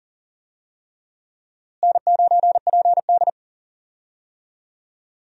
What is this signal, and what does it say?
Morse code, heard as a keyed tone, about 40 wpm, 700 Hz. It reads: N9PD
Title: USA Call Signs - 4 Characters - Repeated 3x - 40wpm